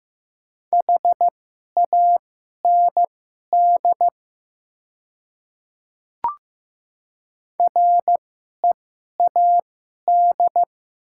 Morse code, heard as a keyed tone, about 15 wpm, spaced as for 11 wpm, 700 Hz. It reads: HAND READ
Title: 2x Character Spacing - Top 200 Words - 15wpm